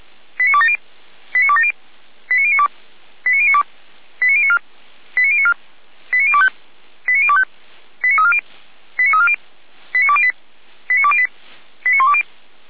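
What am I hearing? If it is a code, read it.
Five-tone selective call (ZVEI), heard as quick runs of five tones, tones of about 70 ms. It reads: 89270 89270 89092 89092 89094 89094 89025 89025 89340 89340 89208 89208 89120
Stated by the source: ZVEI five-tone selective call 89020, received by SDR.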